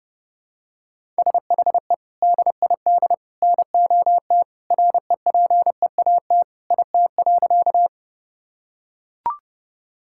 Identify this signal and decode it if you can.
Morse code, heard as a keyed tone, about 30 wpm, 700 Hz. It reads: SHE DID NOT REPEAT IT.
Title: Sentences from Top 1600 Words 30wpm